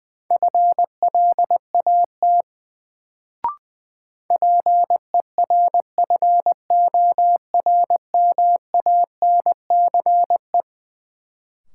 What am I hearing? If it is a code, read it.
Morse code, heard as a keyed tone, about 20 wpm, 700 Hz. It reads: FLAT PERFORMANCE
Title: Sentences from Top 1000 Words - New Word Drill - 20wpm